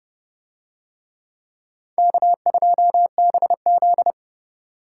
Morse code, heard as a keyed tone, about 26 wpm, 700 Hz. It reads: K2BZ